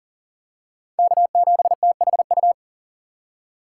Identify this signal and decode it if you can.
Morse code, heard as a keyed tone, about 37 wpm, 700 Hz. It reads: K7THU
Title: USA Call Signs - 4x Character Spacing - 40wpm